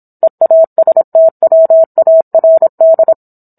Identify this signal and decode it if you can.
Morse code, heard as a keyed tone, about 27 wpm, 650 Hz. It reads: EASTWARD